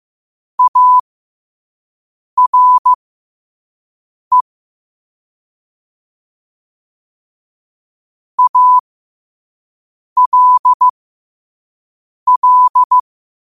Morse code, heard as a keyed tone, about 15 wpm, spaced as for 5 wpm, 1000 Hz. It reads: ARE ALL